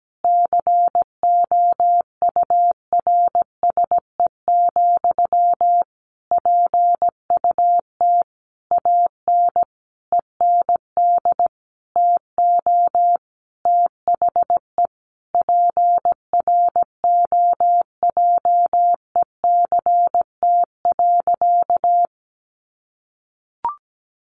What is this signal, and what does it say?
Morse code, heard as a keyed tone, about 17 wpm, 700 Hz. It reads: COURSE, PUT AN END TO THE PROJECT.
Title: Sentences from Top 1400 Words 17wpm